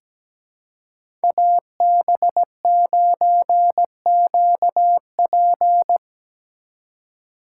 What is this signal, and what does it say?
Morse code, heard as a keyed tone, about 17 wpm, 700 Hz. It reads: AB9QP
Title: USA Call Signs - 5 Characters - 2x Character Spacing - 17wpm